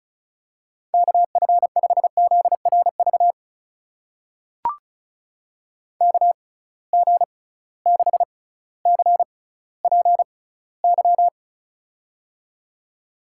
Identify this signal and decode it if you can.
Morse code, heard as a keyed tone, about 34 wpm, spaced as for 18 wpm, 700 Hz. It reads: KF5ZRV KG6CPY
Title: USA Call Signs - 6 Characters - 6x Character Spacing - 35wpm